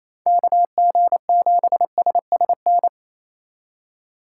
Morse code, heard as a keyed tone, about 28 wpm, 700 Hz. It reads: KG7SSN
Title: USA Call Signs - 6 Characters - 3x Character Spacing - 28wpm